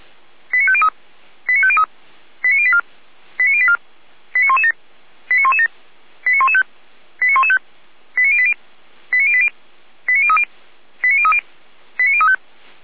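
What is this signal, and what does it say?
Five-tone selective call (ZVEI), heard as quick runs of five tones, tones of about 70 ms. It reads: 89503 89503 89084 89084 89117 89117 89115 89115 89080 89080 89030 89030 89035
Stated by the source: ZVEI five-tone selective call 89094 received via SDR.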